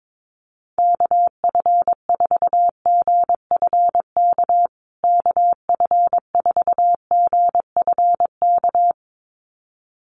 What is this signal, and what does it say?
Morse code, heard as a keyed tone, about 22 wpm, 700 Hz. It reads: KF4GFK KF4GFK
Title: USA Call Signs Repeated 2x 22wpm